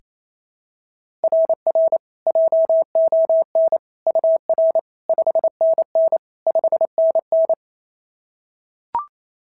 Morse code, heard as a keyed tone, about 28 wpm, 690 Hz. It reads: RR JON UR 5NN 5NN
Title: SOTA QSO Practice 28wpm